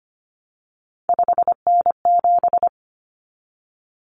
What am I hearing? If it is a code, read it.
Morse code, heard as a keyed tone, about 25 wpm, 700 Hz. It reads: 5N7
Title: Common Cut Numbers - 3-Digit 25wpm